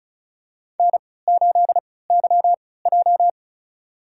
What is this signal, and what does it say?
Morse code, heard as a keyed tone, about 36 wpm, spaced as for 20 wpm, 700 Hz. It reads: N8YJ
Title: USA Call Signs - 4 Characters - 3x Character Spacing - 35wpm